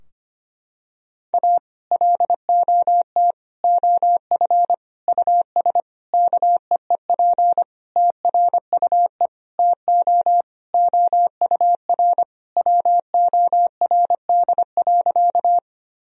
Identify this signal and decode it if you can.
Morse code, heard as a keyed tone, about 25 wpm, 700 Hz. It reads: A LOT OF US KEEP TRUE TO OUR WORD.
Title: Sentences from Top 1000 Words - Review Part 3 - 25wpm